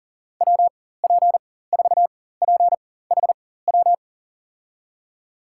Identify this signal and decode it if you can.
Morse code, heard as a keyed tone, about 37 wpm, spaced as for 18 wpm, 700 Hz. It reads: WP4PHW
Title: USA Call Signs - 6 Characters - 4x Character Spacing - 40wpm